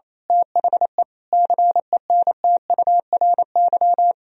Morse code, heard as a keyed tone, about 27 wpm, 700 Hz. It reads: THE CENTURY
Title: Speed-Racing - Sets of 5 Words 28wpm